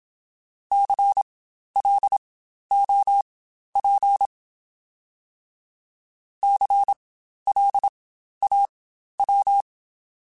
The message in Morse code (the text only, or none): CLOP CLAW